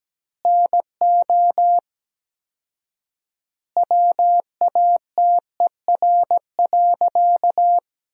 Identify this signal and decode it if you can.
Morse code, heard as a keyed tone, about 17 wpm, 700 Hz. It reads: NO WATER.
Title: 4x Word-Spacing - Sentences from Top 100 Words - 17wpm